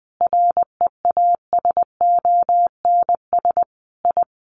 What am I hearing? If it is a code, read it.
Morse code, heard as a keyed tone, about 20 wpm, 700 Hz. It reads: REASONS I
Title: Sets of 5 Words Encore2 20wpm